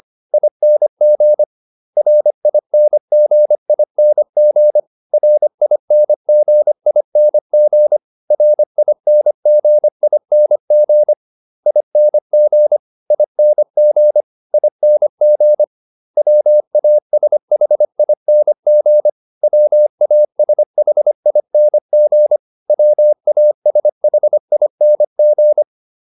Morse code, heard as a keyed tone, about 25 wpm, 600 Hz. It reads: ING RINGING RINGING RINGING ING ING ING WASHING WASHING WASHING